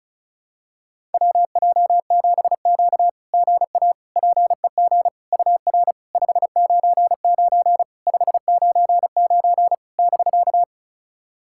Morse code, heard as a keyed tone, about 36 wpm, 700 Hz. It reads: WJ7Q GA PEG UR 599 599 <BK>
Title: SOTA QSO Practice 35wpm